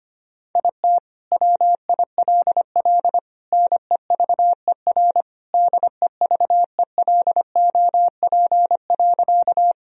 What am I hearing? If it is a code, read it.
Morse code, heard as a keyed tone, about 25 wpm, 700 Hz. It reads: IT WILL NEVER DEVELOP.
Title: Sentences from Top 1000 Words - Review Part 2 - 25wpm